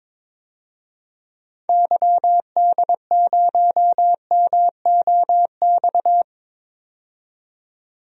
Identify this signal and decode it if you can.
Morse code, heard as a keyed tone, about 22 wpm, 700 Hz. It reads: YD0MOX
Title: DXCC-100 Call Signs - 6 Characters - 22wpm